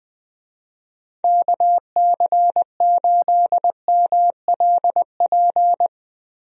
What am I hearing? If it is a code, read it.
Morse code, heard as a keyed tone, about 20 wpm, 700 Hz. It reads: KC8MLP
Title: USA Call Signs - 6 Characters - Repeated 3x - 20wpm